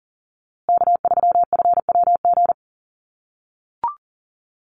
Morse code, heard as a keyed tone, about 40 wpm, 700 Hz. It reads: K3FWG
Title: USA Call Signs Repeated 3x 40wpm